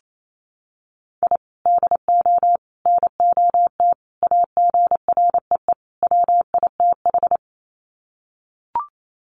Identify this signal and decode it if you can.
Morse code, heard as a keyed tone, about 28 wpm, 700 Hz. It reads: I DO NOT AGREE WITH
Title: Sets of 5 Words Encore 28wpm